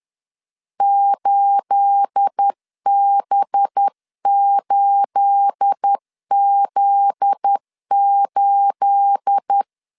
Morse code, heard as a keyed tone, about 10 wpm, 800 Hz. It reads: 8B8Z8
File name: LEZIONI / 9 28 10wpm 2842